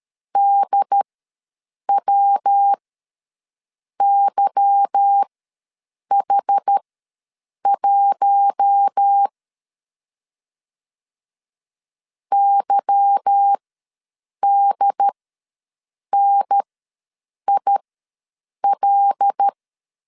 Morse code, heard as a keyed tone, about 13 wpm, spaced as for 7 wpm, 800 Hz. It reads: DWYH1 YDNIL
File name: LEZIONI / 8 YJ45 / 8 Y ws 12wpm 3409